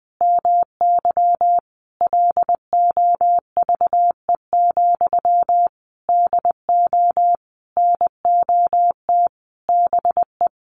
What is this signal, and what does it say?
Morse code, heard as a keyed tone, about 20 wpm, 700 Hz. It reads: MY LOVE, DO NOT BE
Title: Sentences from Top 1500 Words 20wpm